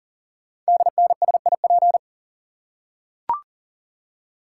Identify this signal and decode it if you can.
Morse code, heard as a keyed tone, about 38 wpm, 700 Hz. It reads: DNSIP
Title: Code Groups - 5 Letters - 40wpm